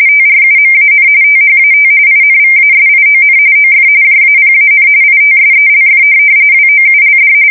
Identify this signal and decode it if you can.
RTTY, about 45.45 baud. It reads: TRUMAN THAT INVASION OF JAPAN ITSELF WOULD
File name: fsk170 / USS-New-Jersey